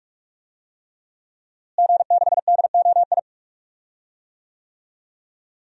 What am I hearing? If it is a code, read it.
Morse code, heard as a keyed tone, about 46 wpm, 700 Hz. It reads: G6DOI
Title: DXCC-100 Call Signs - 5 Characters - Repeated 3x - 45wpm